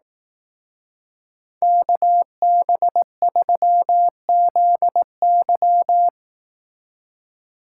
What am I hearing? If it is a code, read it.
Morse code, heard as a keyed tone, about 18 wpm, 700 Hz. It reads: KB3ZY
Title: Speed Racing - USA 2x2 Call Signs 15wpm